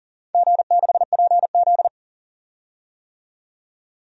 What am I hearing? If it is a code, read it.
Morse code, heard as a keyed tone, about 40 wpm, 690 Hz. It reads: G6PZ
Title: DXCC-100 Call Signs - 4 Characters - 40wpm